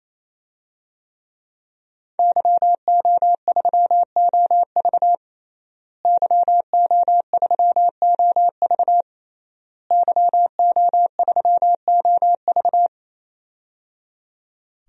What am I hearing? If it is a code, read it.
Morse code, heard as a keyed tone, about 28 wpm, 700 Hz. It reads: YO3OV YO3OV YO3OV
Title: DXCC-100 Call Signs - 5 Characters - Repeated 3x - 28wpm